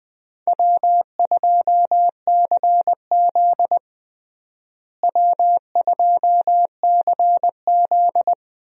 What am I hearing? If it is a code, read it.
Morse code, heard as a keyed tone, about 20 wpm, 700 Hz. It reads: W2CZ W2CZ
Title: USA Call Signs - 4 Characters - Repeated 2x - 20wpm